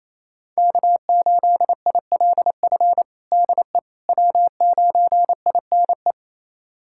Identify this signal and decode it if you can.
Morse code, heard as a keyed tone, about 27 wpm, 700 Hz. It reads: K8ILF DE W9INE
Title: USA Callsign DE Callsign - 5 Characters - 28wpm